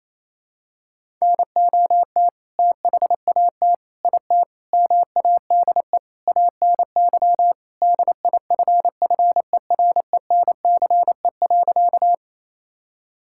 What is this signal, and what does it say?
Morse code, heard as a keyed tone, about 27 wpm, 700 Hz. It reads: NOT THAT IT MADE ANY DIFFERENCE.
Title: Sentences from Top 1000 Words - Review Part 5 - 28wpm